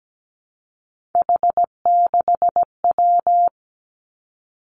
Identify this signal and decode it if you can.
Morse code, heard as a keyed tone, about 17 wpm, 700 Hz. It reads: H6W
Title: Three Challenging Letter-Number Combinations 17wpm